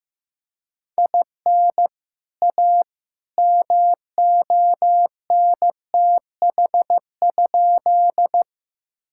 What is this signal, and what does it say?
Morse code, heard as a keyed tone, about 15 wpm, 700 Hz. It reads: IN A MONTH?
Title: Sentences from Top 1000 Words - Review Part 5 - 15wpm